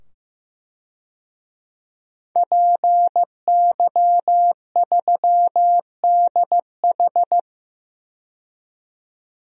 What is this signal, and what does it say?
Morse code, heard as a keyed tone, about 15 wpm, 700 Hz. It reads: PY3DH
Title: DXCC-100 Call Signs - 5 Characters - 15wpm